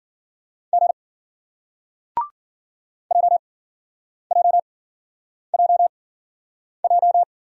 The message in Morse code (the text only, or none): G JJJJ